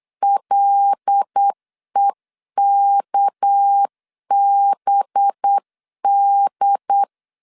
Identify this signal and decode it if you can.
Morse code, heard as a keyed tone, about 8 wpm, 800 Hz. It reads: LEKBD
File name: CW / DET / 1 6 1500C S C 8wpm 5219